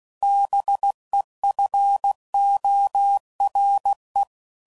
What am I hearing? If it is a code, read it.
Morse code, heard as a keyed tone, about 16 wpm, 800 Hz. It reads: BEFORE